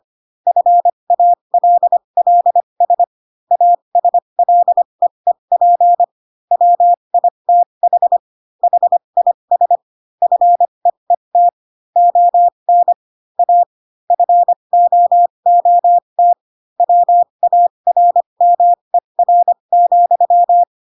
Morse code, heard as a keyed tone, about 25 wpm, spaced as for 22 wpm, 700 Hz. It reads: FALLS ASLEEP WITH HIS FEET ON A FOOT WARMER,
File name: Pinocchio / Pinocchio ch6 18 25 ik1hge